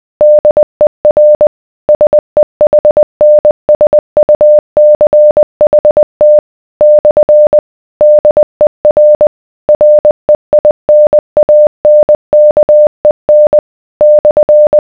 Morse code, heard as a keyed tone, about 20 wpm, 600 Hz. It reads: DER SEHNSUCHT / DER REINANKEN /